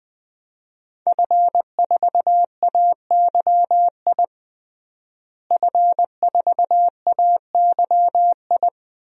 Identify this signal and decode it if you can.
Morse code, heard as a keyed tone, about 20 wpm, 700 Hz. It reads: F4AYI F4AYI
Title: DXCC-100 Call Signs - 5 Characters - Repeated 2x - 20wpm